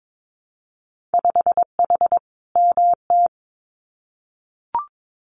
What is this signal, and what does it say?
Morse code, heard as a keyed tone, about 22 wpm, 700 Hz. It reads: 5H MT
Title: Winter Field Day Exchanges Encore 22wpm